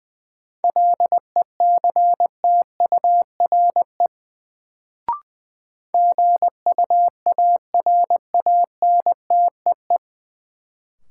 Morse code, heard as a keyed tone, about 20 wpm, 700 Hz. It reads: LECTURE GUARANTEE
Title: Sentences from Top 1600 Words - New Word Drill - 20wpm